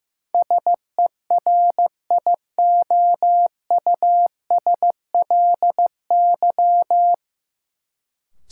Morse code, heard as a keyed tone, about 15 wpm, 700 Hz. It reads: SERIOUSLY
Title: Sentences from Top 1200 Words - New Word Drill Alt - 15wpm